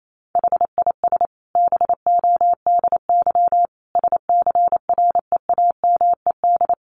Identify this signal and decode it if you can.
Morse code, heard as a keyed tone, about 28 wpm, 700 Hz. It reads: HIS BODY SCREAMED
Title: A Sound Of Thunder 28wpm